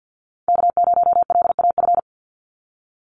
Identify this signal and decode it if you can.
Morse code, heard as a keyed tone, about 48 wpm, 700 Hz. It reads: K0LAF
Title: NA SOTA Chaser Callsigns 50wpm